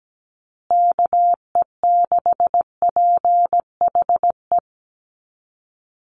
Morse code, heard as a keyed tone, about 17 wpm, 700 Hz. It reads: KE6PHE
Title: USA Call Signs Repeated 3x 17wpm